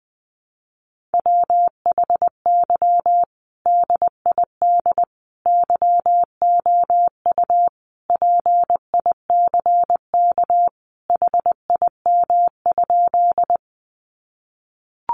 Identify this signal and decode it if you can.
Morse code, heard as a keyed tone, about 20 wpm, 700 Hz. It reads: WHY DID YOU PICK HIM?
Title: Sentences from Top 1000 Words 20wpm